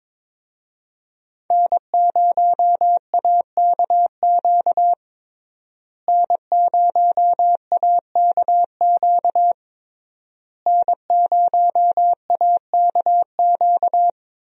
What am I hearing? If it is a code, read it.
Morse code, heard as a keyed tone, about 22 wpm, 700 Hz. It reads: N0AKQ N0AKQ N0AKQ
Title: USA Call Signs - 5 Characters - Repeated 3x - 22wpm